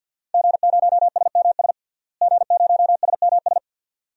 Morse code, heard as a keyed tone, about 50 wpm, 700 Hz. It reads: G0SMS G0SMS
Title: DXCC-100 Call Signs - 5 Characters - Repeated 2x - 50wpm